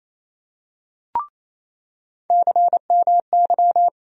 Morse code, heard as a keyed tone, about 27 wpm, 700 Hz. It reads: CMY
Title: USA Call Sign Postfixes - 3 Letters 28wpm